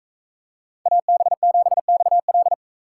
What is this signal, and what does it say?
Morse code, heard as a keyed tone, about 40 wpm, 700 Hz. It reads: AB7XL